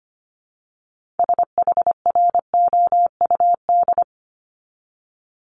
Morse code, heard as a keyed tone, about 25 wpm, 700 Hz. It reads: SHROUD
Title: Multi-letter Phonemes 25wpm